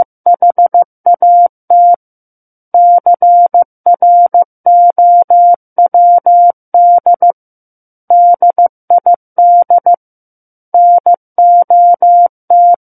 Morse code, heard as a keyed tone, about 15 wpm, 700 Hz. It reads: HAT CROWD DID NOT
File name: Chapter 6 - 15 WPM